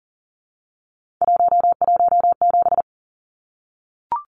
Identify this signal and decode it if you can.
Morse code, heard as a keyed tone, about 40 wpm, 700 Hz. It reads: 117
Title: Numbers - 3-Digit 40wpm